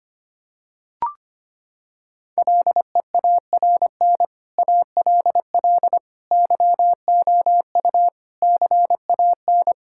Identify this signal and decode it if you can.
Morse code, heard as a keyed tone, about 25 wpm, 700 Hz. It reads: LEARN ALL YOU CAN
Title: Sentences from Top 1000 Words - Review Part 5 - 25wpm